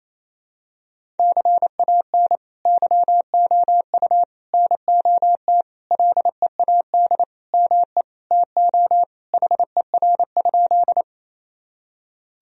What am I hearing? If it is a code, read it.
Morse code, heard as a keyed tone, about 28 wpm, 700 Hz. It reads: CAN YOU NOT LEAD ME TO HER?
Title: Sentences from Top 1000 Words - Review Part 2 - 28wpm